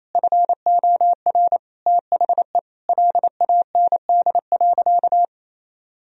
Morse code, heard as a keyed tone, about 28 wpm, 700 Hz. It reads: FOR THE LAND.
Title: Sentences from Top 1000 Words - Review Part 1 - 28wpm